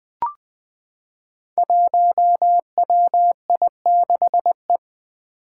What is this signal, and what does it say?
Morse code, heard as a keyed tone, about 20 wpm, 700 Hz. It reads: 1WI6E
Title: Five Challenging Letter-Number Combinations 20wpm